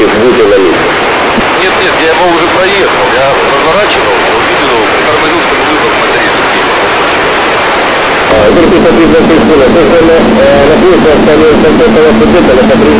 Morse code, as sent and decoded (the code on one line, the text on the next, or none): none